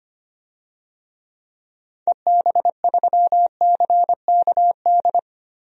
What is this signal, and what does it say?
Morse code, heard as a keyed tone, about 25 wpm, 690 Hz. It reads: EB3CKD
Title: DXCC-100 Call Signs - 6 Characters - 25wpm